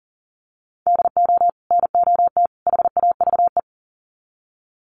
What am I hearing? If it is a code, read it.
Morse code, heard as a keyed tone, about 40 wpm, 700 Hz. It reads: DO NOT HAVE
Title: Sets of 3 Words 40wpm